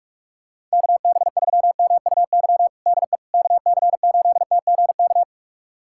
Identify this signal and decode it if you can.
Morse code, heard as a keyed tone, about 46 wpm, 700 Hz. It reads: KB3MUY DE KC8TGK
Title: USA Callsign DE Callsign - 6 Characters - 45wpm